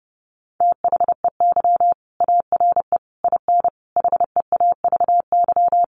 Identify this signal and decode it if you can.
Morse code, heard as a keyed tone, about 30 wpm, 700 Hz. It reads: THEY ARE IN HEAVY
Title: Sentences from Top 1300 Words 30wpm